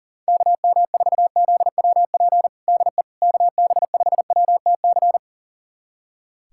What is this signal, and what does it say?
Morse code, heard as a keyed tone, about 40 wpm, 700 Hz. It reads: KM4ZWP DE KB5WTC
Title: USA Callsign DE Callsign 40wpm